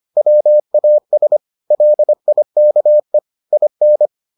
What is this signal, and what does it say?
Morse code, heard as a keyed tone, about 25 wpm, 600 Hz. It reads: WAS LIKE IN